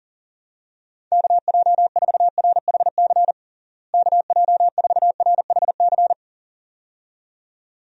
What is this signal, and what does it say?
Morse code, heard as a keyed tone, about 40 wpm, 700 Hz. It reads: KJ4RHC KJ4RHC
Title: USA Call Signs - 6 Characters - Repeated 2x - 40wpm